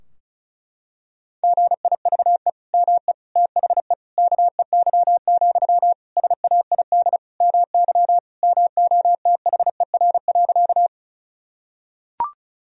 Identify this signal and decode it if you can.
Morse code, heard as a keyed tone, about 36 wpm, 700 Hz. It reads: GIVE ME THE KEY, SAID MY MOTHER.
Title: Sentences from Top 1000 Words - Review Part 4 - 35wpm